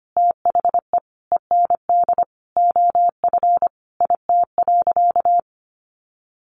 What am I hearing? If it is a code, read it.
Morse code, heard as a keyed tone, about 25 wpm, 700 Hz. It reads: THE END OF IT.
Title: Sentences from Top 1200 Words 25wpm